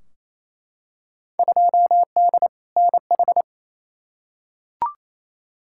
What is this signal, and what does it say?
Morse code, heard as a keyed tone, about 28 wpm, 700 Hz. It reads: 2D NH
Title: Field Day Exchanges Encore - 28wpm